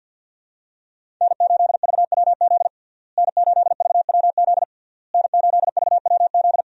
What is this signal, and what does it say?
Morse code, heard as a keyed tone, about 52 wpm, 700 Hz. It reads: N8VWZ N8VWZ N8VWZ
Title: USA Call Signs - 5 Characters - Repeated 3x - 50wpm